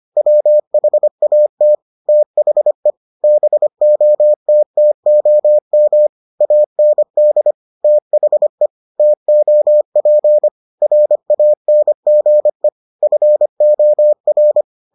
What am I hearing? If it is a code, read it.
Morse code, heard as a keyed tone, about 25 wpm, 600 Hz. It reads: WHAT THE BOTTOM AND THE TOP RANGE FOR